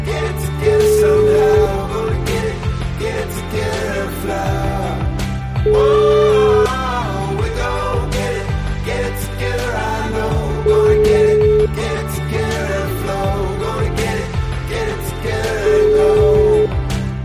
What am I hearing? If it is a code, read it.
Ringback tone (Europe, 425 Hz), with noise.